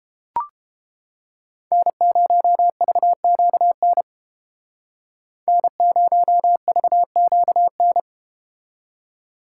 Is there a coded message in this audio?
Yes: Morse code from 1.7 to 8.0 s, heard as a keyed tone, about 30 wpm, 700 Hz. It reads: N0VQN N0VQN